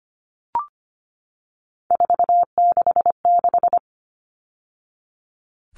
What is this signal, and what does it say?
Morse code, heard as a keyed tone, about 25 wpm, 700 Hz. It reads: 466
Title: Numbers - 3-Digit 25wpm